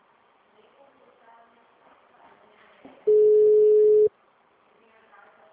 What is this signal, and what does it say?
Ringback tone (Europe, 425 Hz), with speech.